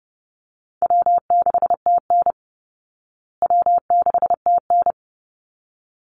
Morse code, heard as a keyed tone, about 30 wpm, 700 Hz. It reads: W6TN W6TN
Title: Easy Call Signs Repeated 2x - 30wpm